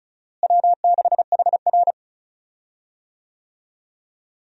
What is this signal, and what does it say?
Morse code, heard as a keyed tone, about 36 wpm, 700 Hz. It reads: W6HR
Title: USA Call Signs - 4 Characters - Repeated 3x - 35wpm